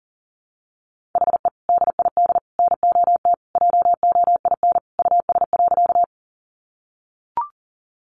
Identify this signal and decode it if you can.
Morse code, heard as a keyed tone, about 40 wpm, 700 Hz. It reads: HE DID NOT JOIN US.
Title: Sentences from Top 1400 Words 40wpm